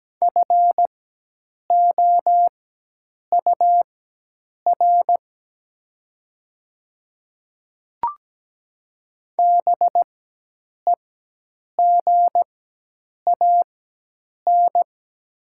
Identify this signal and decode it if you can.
Morse code, heard as a keyed tone, about 17 wpm, spaced as for 8 wpm, 700 Hz. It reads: FOUR BEGAN
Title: 4x Character Spacing - Top 500 Words - 17wpm